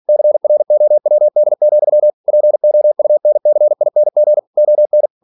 Morse code, heard as a keyed tone, about 46 wpm, 600 Hz. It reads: CROWD, POUNCING ON
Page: Never 47wpm